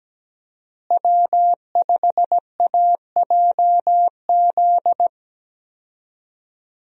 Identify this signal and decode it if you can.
Morse code, heard as a keyed tone, about 17 wpm, 700 Hz. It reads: W5AJZ
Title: USA Call Signs - 5 Characters - Repeated 3x - 17wpm